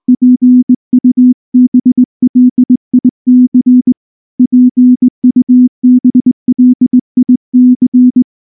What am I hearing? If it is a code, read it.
Morse code, heard as a keyed tone, about 20 wpm, 250 Hz. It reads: PUBLIC PUBLIC
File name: morse / morse-training-20WPM-4